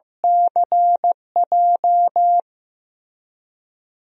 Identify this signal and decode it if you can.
Morse code, heard as a keyed tone, about 16 wpm, 700 Hz. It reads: CJ
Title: DXCC-100 Call Signs - 5 Characters - Repeated 2x - 15wpm